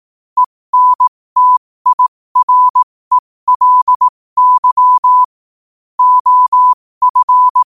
Morse code, heard as a keyed tone, about 18 wpm, spaced as for 15 wpm, 1000 Hz. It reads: ENTIRELY OF
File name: qotdfiles / 15wpm